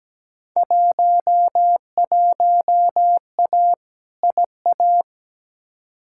Morse code, heard as a keyed tone, about 17 wpm, 700 Hz. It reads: 11A IA
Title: Field Day Exchanges Encore - 17wpm